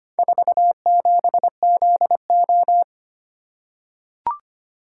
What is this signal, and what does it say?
Morse code, heard as a keyed tone, about 25 wpm, 700 Hz. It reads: 47ZO
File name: practice / 25wpm / Four Letter-Number Combinations - 25wpm